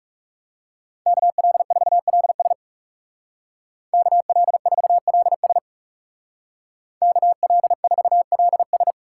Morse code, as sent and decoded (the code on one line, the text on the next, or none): -.- .-.. ....- .-.. ... / -.- .-.. ....- .-.. ... / -.- .-.. ....- .-.. ...
KL4LS KL4LS KL4LS